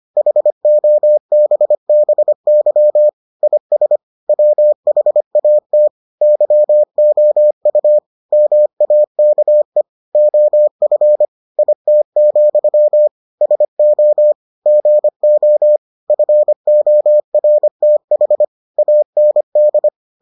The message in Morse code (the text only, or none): HOBBY IS WHAT YOU MAKE OF IT, SO GO FORTH AND